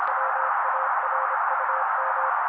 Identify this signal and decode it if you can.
Morse code, heard as a keyed tone, about 27 wpm, 530 Hz. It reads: WARUM
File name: daily / 25wpm noise